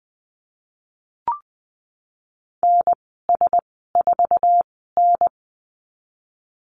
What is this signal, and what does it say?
Morse code, heard as a keyed tone, about 20 wpm, spaced as for 14 wpm, 700 Hz. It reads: NS4N
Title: 147 Lesson 24g - Letter y - Call Signs 20wpm